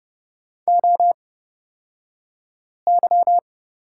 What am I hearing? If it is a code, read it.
Morse code, heard as a keyed tone, about 30 wpm, spaced as for 5 wpm, 700 Hz. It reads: OY
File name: Lesson 37d - Letter x - Two Characters 30wpm